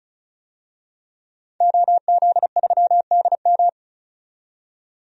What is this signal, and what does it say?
Morse code, heard as a keyed tone, about 36 wpm, 700 Hz. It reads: OZ3DM
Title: DXCC-100 Call Signs - 5 Characters - 35wpm